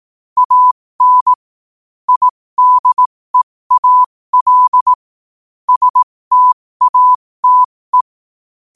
Morse code, heard as a keyed tone, about 18 wpm, spaced as for 15 wpm, 1000 Hz. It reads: AN IDEAL STATE
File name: qotdfiles / 15wpm